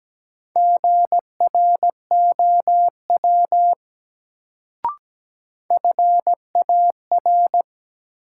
Morse code, heard as a keyed tone, about 17 wpm, 700 Hz. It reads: GROW FAR
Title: Sentences from Top 300 Words - New Word Drill - 17wpm